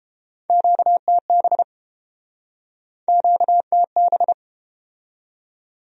Morse code, heard as a keyed tone, about 31 wpm, 700 Hz. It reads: QTB QTB